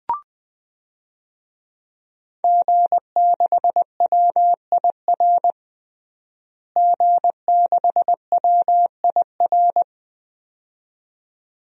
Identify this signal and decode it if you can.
Morse code, heard as a keyed tone, about 20 wpm, 700 Hz. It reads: G6WIR G6WIR
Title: DXCC-100 Call Signs - 5 Characters - Repeated 2x - 20wpm